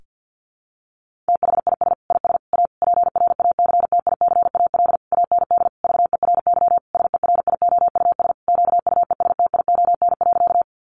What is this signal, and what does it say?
Morse code, heard as a keyed tone, about 50 wpm, 700 Hz. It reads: THIS IS A PRACTICAL AND VERY SERIOUS QUESTION.
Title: Sentences from Top 1400 Words 50wpm